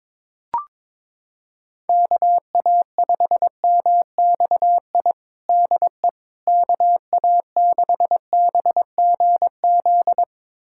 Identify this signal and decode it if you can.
Morse code, heard as a keyed tone, about 22 wpm, 700 Hz. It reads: KA5MXI DE KA6BGZ
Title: USA Callsign DE Callsign - 6 Characters - 22wpm